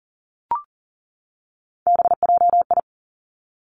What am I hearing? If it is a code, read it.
Morse code, heard as a keyed tone, about 42 wpm, 700 Hz. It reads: BJI
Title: Three Challenging Letter-Number Combinations 40wpm